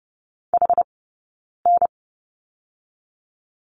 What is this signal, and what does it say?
Morse code, heard as a keyed tone, about 30 wpm, 700 Hz. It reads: H N
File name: Lesson 19d - Letter w - Two Characters 30wpm